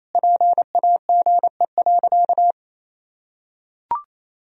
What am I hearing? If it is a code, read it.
Morse code, heard as a keyed tone, about 28 wpm, 700 Hz. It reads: PAGE.
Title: Sentences from Top 1000 Words - Review Part 1 - 28wpm